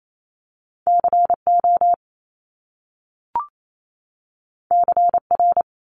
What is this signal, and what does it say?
Morse code, heard as a keyed tone, about 28 wpm, 700 Hz. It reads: CO CR
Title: Top Two Letter Combinations in Words 28wpm